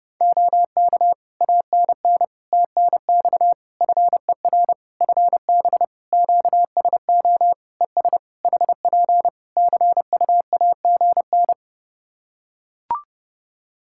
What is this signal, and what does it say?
Morse code, heard as a keyed tone, about 30 wpm, 700 Hz. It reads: OK ANN TNX FER FB QSO ES HP CUAGN
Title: Practice QSO - Exchange 4 - Ending - 30wpm